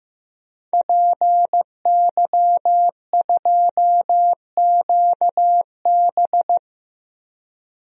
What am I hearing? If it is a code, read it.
Morse code, heard as a keyed tone, about 15 wpm, 700 Hz. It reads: PY2QB
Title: DXCC-100 Call Signs - 5 Characters - Repeated 2x - 15wpm